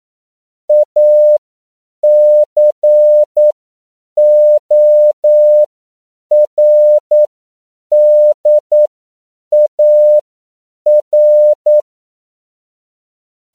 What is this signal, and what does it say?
Morse code, heard as a keyed tone, about 9 wpm, spaced as for 7 wpm, 600 Hz. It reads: ACORDAR
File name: CW / 10PPM